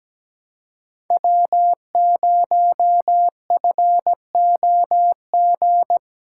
Morse code, heard as a keyed tone, about 17 wpm, 700 Hz. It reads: W0FOG
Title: USA Call Signs - 5 Characters - 4x Character Spacing - 17wpm